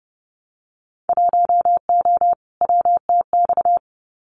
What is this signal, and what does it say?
Morse code, heard as a keyed tone, about 30 wpm, 700 Hz. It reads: 1O WTX
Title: Winter Field Day Exchanges Encore 30wpm